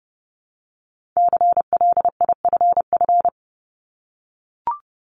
Morse code, heard as a keyed tone, about 30 wpm, 700 Hz. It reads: CLIFF
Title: Multi-letter Phonemes Repeated 3x 30wpm